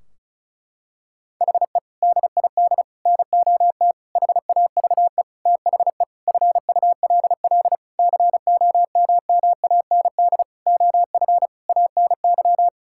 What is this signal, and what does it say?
Morse code, heard as a keyed tone, about 36 wpm, 700 Hz. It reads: HE DID NOT HAVE THE FULL COMMAND OF ANY
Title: Sentences from Top 1600 Words 35wpm